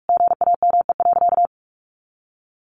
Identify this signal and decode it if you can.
Morse code, heard as a keyed tone, about 44 wpm, 700 Hz. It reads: GAME.
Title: Sentences from Top 500 Words 45wpm